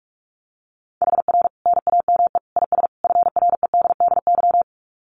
Morse code, heard as a keyed tone, about 44 wpm, 700 Hz. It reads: HR NAME IS FREDDY
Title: Common QSO Phrases 45wpm